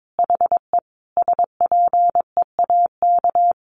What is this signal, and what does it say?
Morse code, heard as a keyed tone, about 22 wpm, 700 Hz. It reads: HE SPEAK
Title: Sentences from Top 1500 Words 22wpm